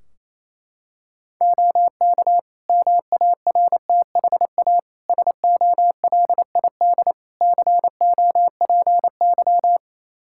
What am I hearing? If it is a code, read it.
Morse code, heard as a keyed tone, about 28 wpm, 700 Hz. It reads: OK MARTHA SOLID COPY
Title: Practice QSO - Exchange 3 - More Info - 28wpm